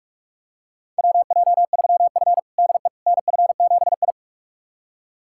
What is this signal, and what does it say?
Morse code, heard as a keyed tone, about 42 wpm, 700 Hz. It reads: WJ3F DE NF7I
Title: USA Callsign DE Callsign - 4 Characters - 45wpm